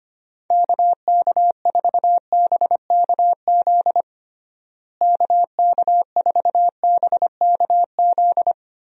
Morse code, heard as a keyed tone, about 25 wpm, 700 Hz. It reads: KK4BKZ KK4BKZ
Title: USA Call Signs - 6 Characters - Repeated 3x - 25wpm